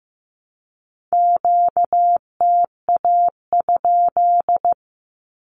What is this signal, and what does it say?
Morse code, heard as a keyed tone, about 15 wpm, 700 Hz. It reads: QTA?
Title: Q-Codes Repeated 3x 15wpm